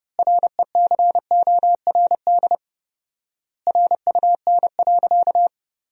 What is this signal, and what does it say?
Morse code, heard as a keyed tone, about 30 wpm, 700 Hz. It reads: RECORD RUN.
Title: 4x Word-Spacing - Sentences from Top 500 Words - 30wpm